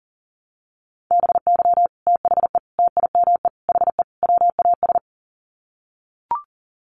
Morse code, heard as a keyed tone, about 40 wpm, 700 Hz. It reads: BY THE TIME HE WAS
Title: Sets of 5 Words Encore 40wpm